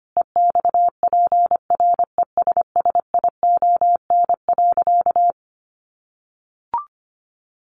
Morse code, heard as a keyed tone, about 25 wpm, 700 Hz. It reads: EXPRESSION.
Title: Sentences from Top 1200 Words 25wpm